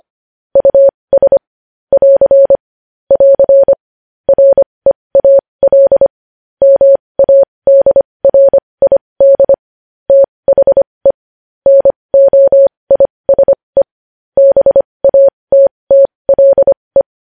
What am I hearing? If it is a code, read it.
Morse code, heard as a keyed tone, about 25 wpm, spaced as for 20 wpm, 550 Hz. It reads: US <AR> <AR> REAL MADRID THE NOISE BATTLE
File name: podcast / News headlines-1x-25x20